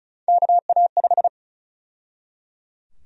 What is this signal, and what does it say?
Morse code, heard as a keyed tone, about 34 wpm, 700 Hz. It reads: KA5
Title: USA Call Sign Prefixes - 35wpm